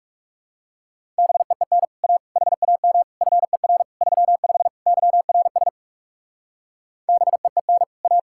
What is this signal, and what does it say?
Morse code, heard as a keyed tone, about 37 wpm, 700 Hz. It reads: BEEN A HAM FER 35 YRS BEEN A
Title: Speed-Racing Common QSO Phrases - 30wpm